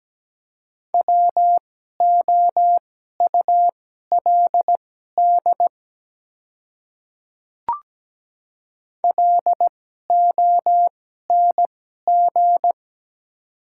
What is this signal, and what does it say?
Morse code, heard as a keyed tone, about 17 wpm, spaced as for 12 wpm, 700 Hz. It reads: WOULD LONG
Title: 2x Character Spacing - Top 100 Words - 17wpm